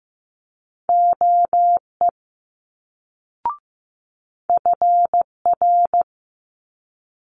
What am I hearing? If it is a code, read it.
Morse code, heard as a keyed tone, about 15 wpm, 700 Hz. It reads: OE FR
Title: Multi-letter Phonemes Quick Fire 15wpm